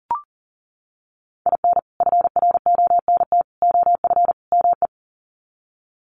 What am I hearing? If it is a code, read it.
Morse code, heard as a keyed tone, about 40 wpm, 700 Hz. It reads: IN FRONT OF ME
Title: Sets of 4 Words 40wpm